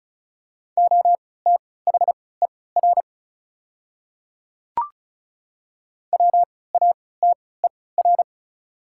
Morse code, heard as a keyed tone, about 35 wpm, spaced as for 20 wpm, 700 Hz. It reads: OTHER WATER
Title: 3x Character Spacing - Top 100 Words - 35wpm